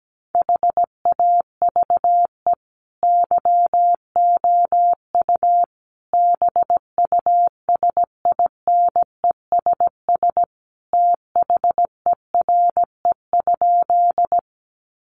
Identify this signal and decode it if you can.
Morse code, heard as a keyed tone, about 17 wpm, 700 Hz. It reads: HAVE YOU BUSINESS THERE?
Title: Sentences from Top 600 Words 17wpm